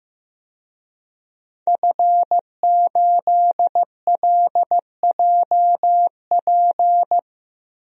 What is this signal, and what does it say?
Morse code, heard as a keyed tone, about 15 wpm, 700 Hz. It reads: F8LJP
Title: DXCC-100 Call Signs - 5 Characters - 15wpm